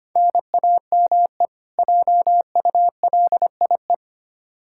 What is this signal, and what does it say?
Morse code, heard as a keyed tone, about 25 wpm, 700 Hz. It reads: NAME JULIE
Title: K1USN SST QSO Practice 25wpm